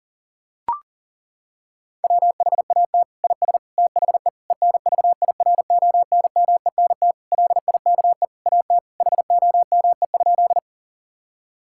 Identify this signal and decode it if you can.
Morse code, heard as a keyed tone, about 40 wpm, 700 Hz. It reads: WHAT IS THE ENVIRONMENT LIKE AT HOME?
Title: Sentences from Top 1000 Words - Review Part 5 - 40wpm